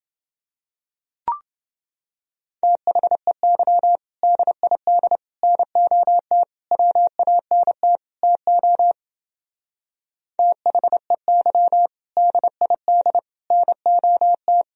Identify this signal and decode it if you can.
Morse code, heard as a keyed tone, about 28 wpm, 700 Hz. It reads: THEY DID NOT WANT TO THEY DID NOT